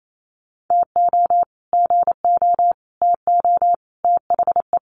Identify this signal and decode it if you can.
Morse code, heard as a keyed tone, about 28 wpm, 700 Hz. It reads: TO GO TO THE
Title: Sets of 4 Words 28wpm